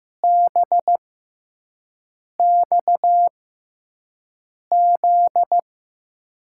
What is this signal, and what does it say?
Morse code, heard as a keyed tone, about 15 wpm, spaced as for 5 wpm, 700 Hz. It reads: BXZ
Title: USA Call Signs - 6x Character Spacing - 15wpm